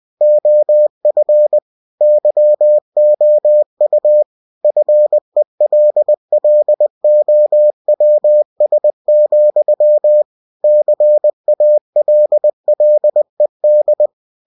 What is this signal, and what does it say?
Morse code, heard as a keyed tone, about 20 wpm, 600 Hz. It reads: OF YOU FELLOWS, CALLED
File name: winnie-20wpm